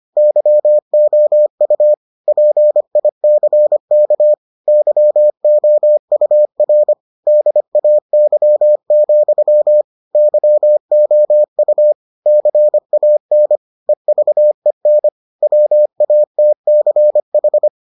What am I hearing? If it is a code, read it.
Morse code, heard as a keyed tone, about 25 wpm, 600 Hz. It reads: YOU PICK YOUR DAY, YOU CAN EVEN WATCH